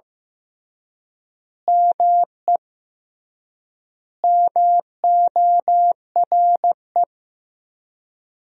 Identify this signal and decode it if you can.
Morse code, heard as a keyed tone, about 15 wpm, 700 Hz. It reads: ME MORE
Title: 3x Word-Spacing - Sentences from Top 200 Words - 15wpm